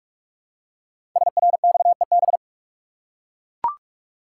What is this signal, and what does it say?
Morse code, heard as a keyed tone, about 44 wpm, 690 Hz. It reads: SRXEB
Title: Code Groups - 5 Letters - 45wpm